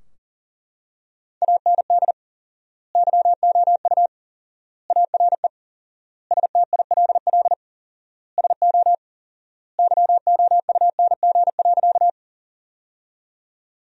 Morse code, heard as a keyed tone, about 37 wpm, 700 Hz. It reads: AND YOU ARE STILL SO YOUNG.
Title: 4x Word-Spacing - Sentences from Top 400 Words - 40wpm